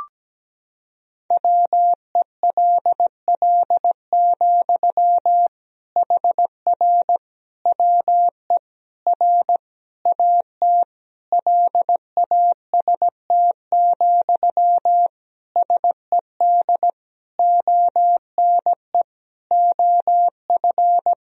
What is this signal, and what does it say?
Morse code, heard as a keyed tone, about 17 wpm, 700 Hz. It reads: WELL, HR WE R AT LAST, SED ONE OF
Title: Sentences from Top 300 Words and Abbreviations 17wpm